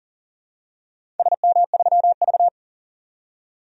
Morse code, heard as a keyed tone, about 40 wpm, 700 Hz. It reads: SM3V
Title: DXCC-100 Call Signs - 4 Characters - 40wpm